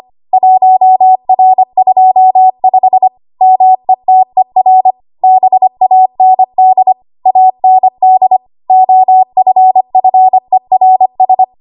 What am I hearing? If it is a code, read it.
Morse code, heard as a keyed tone, about 25 wpm, 750 Hz. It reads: 1R25 METER BAND AND OFFERS
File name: Archive / 25 WPM / 25WPM